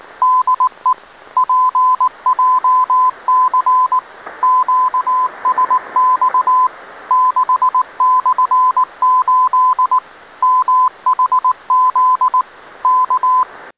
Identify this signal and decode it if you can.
Morse code, heard as a keyed tone, about 19 wpm, 1020 Hz. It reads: DE PJC QSX 6/8 MHZ K